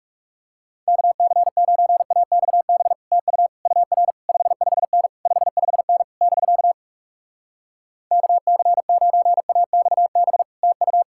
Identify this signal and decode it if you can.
Morse code, heard as a keyed tone, about 40 wpm, 700 Hz. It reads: KC9AXB TU UR 55N 55N <BK> KC9AXB TU